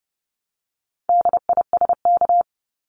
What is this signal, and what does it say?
Morse code, heard as a keyed tone, about 30 wpm, 700 Hz. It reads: DISK
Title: Top 751-1000 Words 30wpm